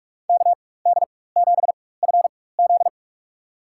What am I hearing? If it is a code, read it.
Morse code, heard as a keyed tone, about 46 wpm, spaced as for 21 wpm, 700 Hz. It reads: KD7FZ
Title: USA Call Signs - 5 Characters - 4x Character Spacing - 45wpm